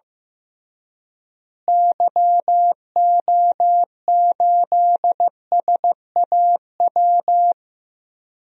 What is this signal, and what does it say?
Morse code, heard as a keyed tone, about 15 wpm, 700 Hz. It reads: YO8SAW
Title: DXCC-100 Call Signs - 6 Characters - Repeated 3x - 15wpm